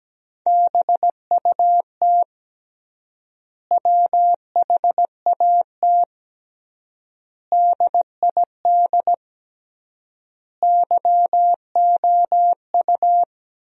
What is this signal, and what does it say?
Morse code, heard as a keyed tone, about 17 wpm, 700 Hz. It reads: BUT WHAT DID YOU
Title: 3x Word-Spacing - Sentences from Top 100 Words - 17wpm